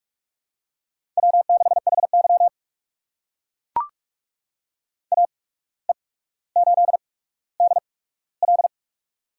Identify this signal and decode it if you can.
Morse code, heard as a keyed tone, about 42 wpm, spaced as for 18 wpm, 700 Hz. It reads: W6HY AE8DL
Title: USA Call Signs - 8x Character Spacing - 45wpm